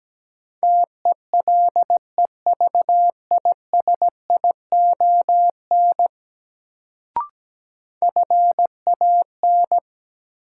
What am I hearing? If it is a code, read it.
Morse code, heard as a keyed tone, about 17 wpm, 700 Hz. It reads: TELEVISION FAN
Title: Sentences from Top 1000 Words - Review Part 4 - New Word Drill - 17wpm